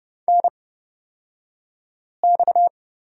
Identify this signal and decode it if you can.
Morse code, heard as a keyed tone, about 30 wpm, spaced as for 5 wpm, 700 Hz. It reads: NX
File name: Lesson 37d - Letter x - Two Characters 30wpm